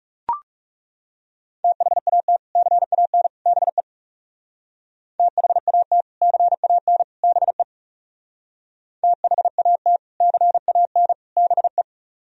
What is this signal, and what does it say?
Morse code, heard as a keyed tone, about 40 wpm, 700 Hz. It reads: THAT CAN BE THAT CAN BE THAT CAN BE